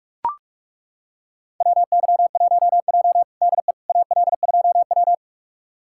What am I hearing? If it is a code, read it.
Morse code, heard as a keyed tone, about 41 wpm, 700 Hz. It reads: WY1J DE AL2W
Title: USA Callsign DE Callsign - 4 Characters - 45wpm